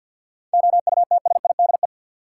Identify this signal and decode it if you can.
Morse code, heard as a keyed tone, about 50 wpm, 700 Hz. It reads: OUTSIDE
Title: Sentences from Top 900 Words - New Word Drill - 50wpm